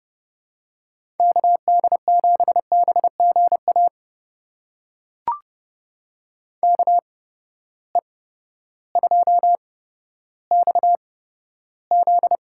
Morse code, heard as a keyed tone, about 31 wpm, spaced as for 14 wpm, 700 Hz. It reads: KD7BGA KE2XZ
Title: USA Call Signs - 8x Character Spacing - 30wpm